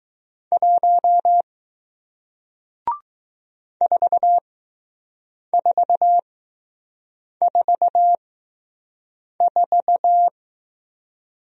Morse code, heard as a keyed tone, about 20 wpm, spaced as for 6 wpm, 700 Hz. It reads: 1 4444